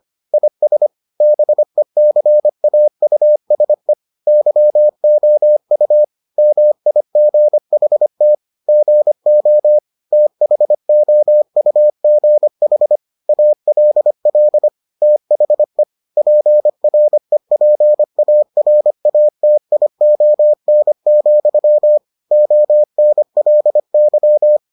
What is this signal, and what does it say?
Morse code, heard as a keyed tone, about 25 wpm, 600 Hz. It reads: IS BECAUSE YOU MIGHT GO THOUGH ALL THE PREPARATION, ONLY